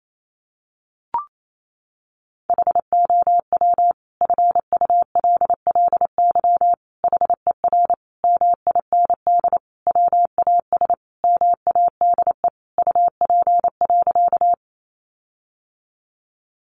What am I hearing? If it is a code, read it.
Morse code, heard as a keyed tone, about 28 wpm, 700 Hz. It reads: HOW FULLY HER MIND WAS MADE UP.
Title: Sentences from Top 1300 Words 28wpm